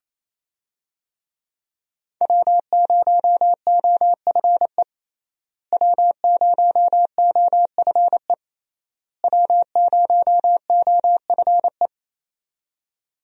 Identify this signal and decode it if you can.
Morse code, heard as a keyed tone, about 27 wpm, 700 Hz. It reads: W0OFE W0OFE W0OFE
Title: USA Call Signs - 5 Characters - Repeated 3x - 28wpm